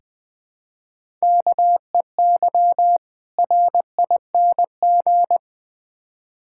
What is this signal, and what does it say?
Morse code, heard as a keyed tone, about 21 wpm, 700 Hz. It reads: KEY RING
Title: Difficult Words with Hints - 20wpm